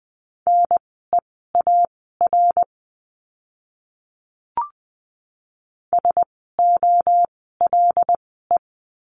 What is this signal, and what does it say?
Morse code, heard as a keyed tone, about 20 wpm, spaced as for 14 wpm, 700 Hz. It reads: NEAR SOLE
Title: 075 Lesson 13f - Letter L - Words 20wpm